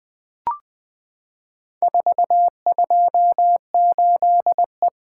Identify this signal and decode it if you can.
Morse code, heard as a keyed tone, about 20 wpm, 700 Hz. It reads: 428E
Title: Four Letter-Number Combinations - 20wpm